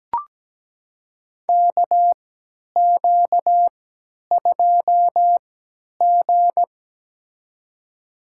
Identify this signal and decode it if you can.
Morse code, heard as a keyed tone, about 17 wpm, spaced as for 10 wpm, 700 Hz. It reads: KQ2G
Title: USA Call Signs - 4 Characters - 3x Character Spacing - 17wpm